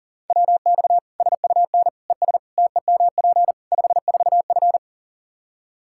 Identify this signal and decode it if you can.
Morse code, heard as a keyed tone, about 37 wpm, 700 Hz. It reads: WX SUN ES TEMP 54F
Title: Practice QSO - Exchange 2 - Info - 40wpm